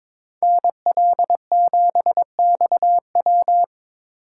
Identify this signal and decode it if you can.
Morse code, heard as a keyed tone, about 25 wpm, 700 Hz. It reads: NL7XW
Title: Speed Racing - USA 2x2 Call Signs 22wpm